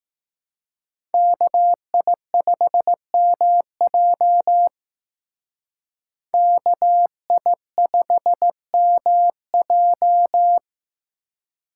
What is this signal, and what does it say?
Morse code, heard as a keyed tone, about 18 wpm, 700 Hz. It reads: KI5MJ KI5MJ
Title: Speed Racing - USA 2x2 Call Signs 15wpm